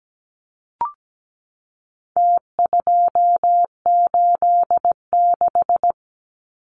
Morse code, heard as a keyed tone, about 17 wpm, 700 Hz. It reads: T286
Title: Numbers - Contest Style - 4-Digit 17wpm